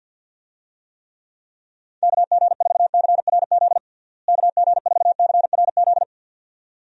Morse code, heard as a keyed tone, about 46 wpm, 700 Hz. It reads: KG4CRZ KG4CRZ
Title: USA Call Signs - 6 Characters - Repeated 2x - 50wpm